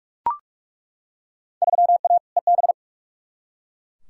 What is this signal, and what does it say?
Morse code, heard as a keyed tone, about 41 wpm, 700 Hz. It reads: 3A EB
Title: Field Day Exchanges Encore - 45wpm